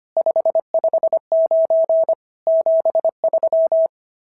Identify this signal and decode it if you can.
Morse code, heard as a keyed tone, about 25 wpm, 650 Hz. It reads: 559 73
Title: SOTA QSO Practice 25wpm